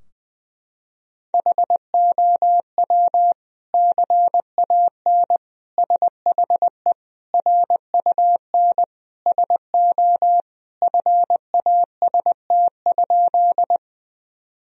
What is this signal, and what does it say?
Morse code, heard as a keyed tone, about 19 wpm, 700 Hz. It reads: HOW CAN SHE RUN SO FAST?
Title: Sentences from Top 1000 Words - Review Part 2 - 20wpm